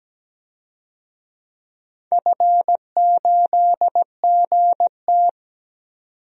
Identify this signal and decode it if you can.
Morse code, heard as a keyed tone, about 17 wpm, 700 Hz. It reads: F8GT
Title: DXCC-100 Call Signs - 4 Characters - 17wpm